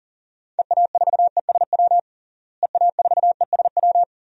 Morse code, heard as a keyed tone, about 40 wpm, 700 Hz. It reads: EA4ESW EA4ESW
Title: DXCC-100 Call Signs - 6 Characters - Repeated 2x - 40wpm